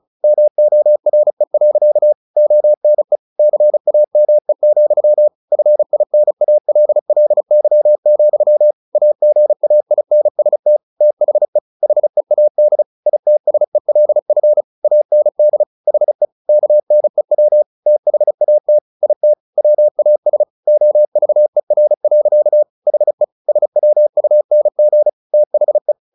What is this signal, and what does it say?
Morse code, heard as a keyed tone, about 34 wpm, 600 Hz. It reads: MORE. ONE CAME, FINALLY, AGAINST THE HEAD ITSELF AND HE KNEW THAT IT WAS OVER. HE SWUNG THE
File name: the old man and the sea 2 35wpm